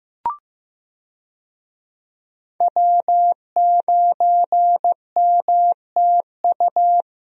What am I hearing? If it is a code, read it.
Morse code, heard as a keyed tone, about 15 wpm, 700 Hz. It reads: W9MTU
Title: USA Call Signs - 5 Characters - Repeated 3x - 15wpm